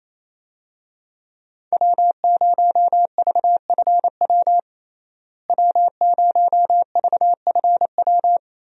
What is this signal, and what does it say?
Morse code, heard as a keyed tone, about 28 wpm, 700 Hz. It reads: W0VFW W0VFW
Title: USA Call Signs - 5 Characters - Repeated 2x - 28wpm